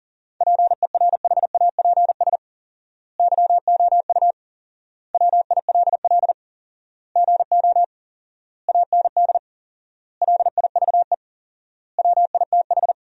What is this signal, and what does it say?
Morse code, heard as a keyed tone, about 40 wpm, 700 Hz. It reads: PERHAPS YOU WILL GO AND LIVE WITH
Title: 4x Word-Spacing - Sentences from Top 500 Words - 40wpm